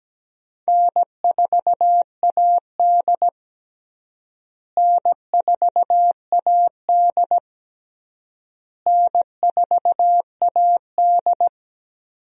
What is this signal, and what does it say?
Morse code, heard as a keyed tone, about 17 wpm, 700 Hz. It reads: N4AD N4AD N4AD
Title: USA Call Signs - 4 Characters - Repeated 3x - 17wpm